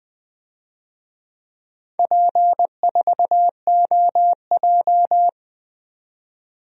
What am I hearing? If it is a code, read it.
Morse code, heard as a keyed tone, about 20 wpm, 700 Hz. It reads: P4OJ
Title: DXCC-100 Call Signs - 4 Characters - 20wpm